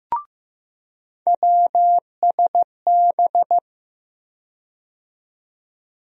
Morse code, heard as a keyed tone, about 15 wpm, 700 Hz. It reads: WSB
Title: USA Call Sign Postfixes - 3 Letters 15wpm